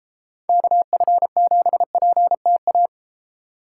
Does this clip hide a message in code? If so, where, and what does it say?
Yes: Morse code from 0.5 to 2.9 s, heard as a keyed tone, about 31 wpm, 700 Hz. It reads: KF7PTA